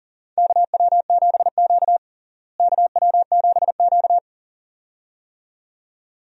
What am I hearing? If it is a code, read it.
Morse code, heard as a keyed tone, about 38 wpm, 700 Hz. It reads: KW7Q KW7Q
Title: USA Call Signs - 4 Characters - Repeated 2x - 40wpm